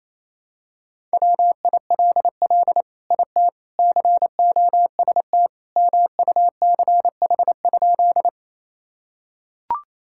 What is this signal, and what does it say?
Morse code, heard as a keyed tone, about 28 wpm, 700 Hz. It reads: WILL IT COST MUCH?
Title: Sentences from Top 1000 Words - Review Part 5 - 28wpm